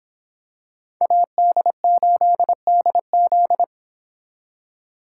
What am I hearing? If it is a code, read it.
Morse code, heard as a keyed tone, about 26 wpm, 700 Hz. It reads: AD8DZ